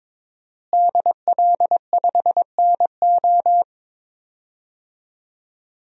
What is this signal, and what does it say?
Morse code, heard as a keyed tone, about 22 wpm, 700 Hz. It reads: DL5NO
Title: DXCC-100 Call Signs - 5 Characters - Repeated 3x - 22wpm